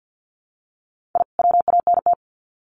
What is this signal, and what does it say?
Morse code, heard as a keyed tone, about 50 wpm, 700 Hz. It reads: I WANT
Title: Sets of 2 Words 50wpm